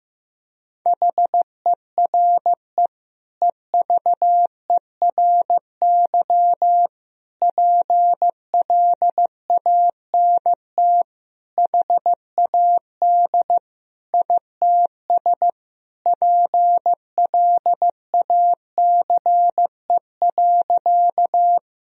Morse code, heard as a keyed tone, about 15 wpm, 700 Hz. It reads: HERE EVERY PLANT HAD ITS PLACE.
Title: Sentences from Top 1000 Words - Review Part 1 - 15wpm